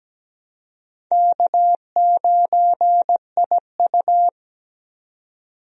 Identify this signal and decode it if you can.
Morse code, heard as a keyed tone, about 17 wpm, 700 Hz. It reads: K9IU
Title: USA Call Signs - 4 Characters - 17wpm